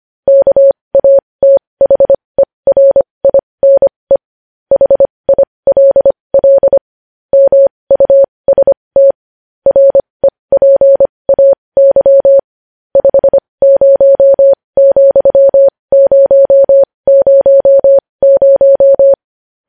Morse code, heard as a keyed tone, about 25 wpm, spaced as for 20 wpm, 550 Hz. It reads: KATHERINE HILL MUST REPAY 50,000
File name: podcast / News headlines-1x-25x20